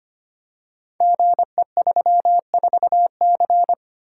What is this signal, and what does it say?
Morse code, heard as a keyed tone, about 25 wpm, 700 Hz. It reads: GE34C
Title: Code Groups - 5 Letter:Numbers - 25wpm